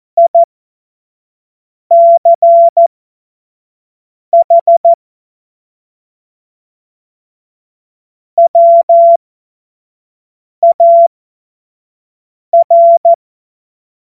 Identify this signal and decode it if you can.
Morse code, heard as a keyed tone, about 14 wpm, spaced as for 5 wpm, 680 Hz. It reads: ICH WAR
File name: Morsekurs / Witze / Witz 18. April 2023 25 F70